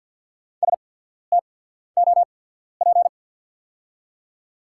Morse code, heard as a keyed tone, about 48 wpm, spaced as for 14 wpm, 690 Hz. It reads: STOP
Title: 8x Character Spacing - Top 500 Words - 50wpm